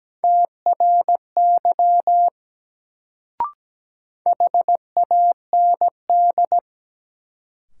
Morse code, heard as a keyed tone, about 17 wpm, 700 Hz. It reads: TRY HAND
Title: Sentences from Top 200 Words - New Word Drill - 17wpm